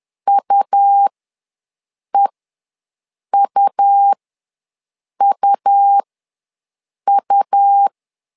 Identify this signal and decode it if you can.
Morse code, heard as a keyed tone, about 11 wpm, spaced as for 6 wpm, 800 Hz. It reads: UEUUU
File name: LEZIONI / 2 UHDN / 2 U ws 10wpm 2017